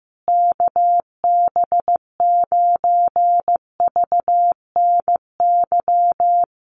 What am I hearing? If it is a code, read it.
Morse code, heard as a keyed tone, about 15 wpm, 700 Hz. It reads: KB9VNY
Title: USA Call Signs Repeated 3x 15wpm